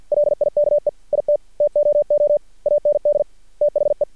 Morse code, heard as a keyed tone, about 50 wpm, 590 Hz. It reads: LIKE IT TOO AND THE
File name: archived / CRSnet